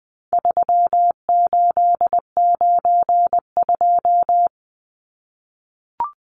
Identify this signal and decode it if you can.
Morse code, heard as a keyed tone, about 20 wpm, 700 Hz. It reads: 3892
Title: Numbers - 4-Digit 20wpm